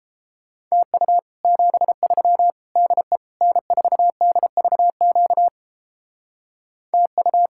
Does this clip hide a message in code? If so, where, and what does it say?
Yes: Morse code from 0.7 to 7.5 s, heard as a keyed tone, about 29 wpm, 700 Hz. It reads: TU 73 DE N4DVQ TU